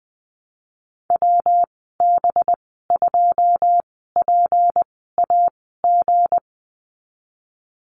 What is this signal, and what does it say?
Morse code, heard as a keyed tone, about 20 wpm, spaced as for 15 wpm, 700 Hz. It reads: WB2PAG
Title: 186 Lesson 30g - Letter b - Call Signs 20wpm